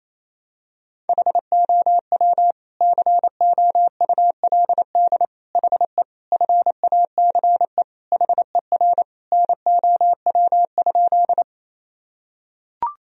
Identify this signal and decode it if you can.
Morse code, heard as a keyed tone, about 28 wpm, 700 Hz. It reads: HOW COULD HE FACE HER NOW?
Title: Sentences from Top 1000 Words - Review Part 2 - 28wpm